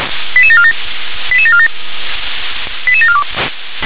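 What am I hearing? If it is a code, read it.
Five-tone selective call (ZVEI), heard as quick runs of five tones, tones of about 70 ms. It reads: 88747 88747 88742
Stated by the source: ZVEI five-tone call received by SDR.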